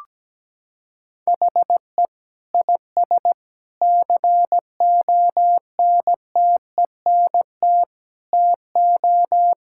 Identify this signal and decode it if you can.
Morse code, heard as a keyed tone, about 17 wpm, 700 Hz. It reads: HE IS CONTENT TO
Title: Sentences from Top 1000 Words - Review Part 6 - 17wpm